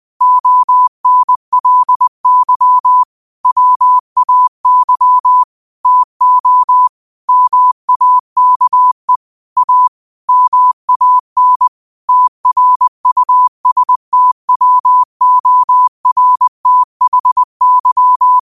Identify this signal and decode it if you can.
Morse code, heard as a keyed tone, about 20 wpm, 1000 Hz. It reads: ONLY WAY TO MAKE A MAN TRUSTWORTHY
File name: qotdfiles / 20wpm